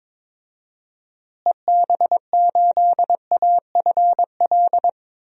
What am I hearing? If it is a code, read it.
Morse code, heard as a keyed tone, about 22 wpm, 700 Hz. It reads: EB8AFL
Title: DXCC-100 Call Signs - 6 Characters - Repeated 2x - 22wpm